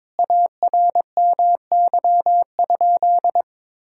Morse code, heard as a keyed tone, about 22 wpm, 700 Hz. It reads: ARMY?
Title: Sentences from Top 1000 Words - Review Part 5 - 22wpm